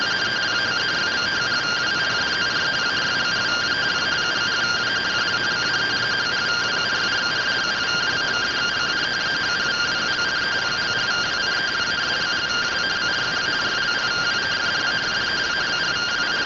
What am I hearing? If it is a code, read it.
Two-tone teleprinter signal, about 170 Hz shift.